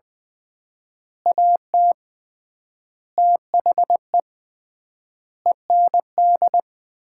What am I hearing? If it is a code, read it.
Morse code, heard as a keyed tone, about 20 wpm, 690 Hz. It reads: AT THE END
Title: 3x Word-Spacing - Sets of 5 Words - 20wpm